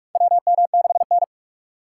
Morse code, heard as a keyed tone, about 46 wpm, 700 Hz. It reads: WM6N
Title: USA Call Signs - 4 Characters - 4x Character Spacing - 45wpm